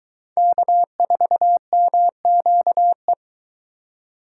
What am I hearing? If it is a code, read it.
Morse code, heard as a keyed tone, about 23 wpm, 700 Hz. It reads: K4MQE